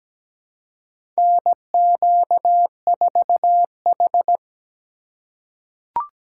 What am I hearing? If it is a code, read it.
Morse code, heard as a keyed tone, about 17 wpm, 700 Hz. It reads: NQ4H
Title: USA Call Signs - 4 Characters - 3x Character Spacing - 17wpm